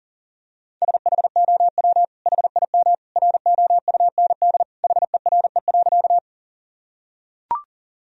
Morse code, heard as a keyed tone, about 40 wpm, 700 Hz. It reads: SHOW HIM ROUND HERE.
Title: Sentences from Top 1000 Words - Review Part 1 - 40wpm